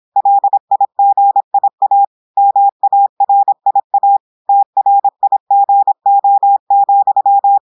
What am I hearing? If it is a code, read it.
Morse code, heard as a keyed tone, about 26 wpm, 800 Hz. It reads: LIGIA MARIA TRIGO,
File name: audios / portugais / portugais texte en clair 1 vitesse 26 mots minutes